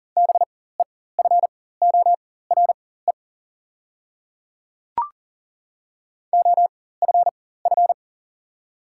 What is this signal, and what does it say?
Morse code, heard as a keyed tone, about 40 wpm, spaced as for 19 wpm, 700 Hz. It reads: BEFORE OFF
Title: 4x Character Spacing - Top 200 Words - 40wpm